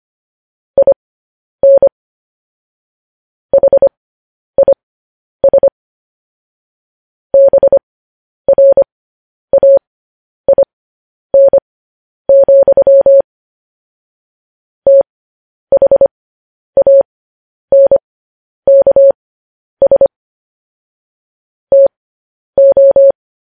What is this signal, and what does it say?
Morse code, heard as a keyed tone, about 25 wpm, spaced as for 10 wpm, 550 Hz. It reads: IN HIS BRAIN, THANKS TO
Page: news-headlines-in-morse-code-at-10-wpm